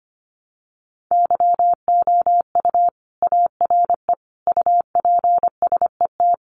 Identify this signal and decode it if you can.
Morse code, heard as a keyed tone, about 25 wpm, 700 Hz. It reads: YOU ARE UPSET
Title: Sentences from Top 800 Words 25wpm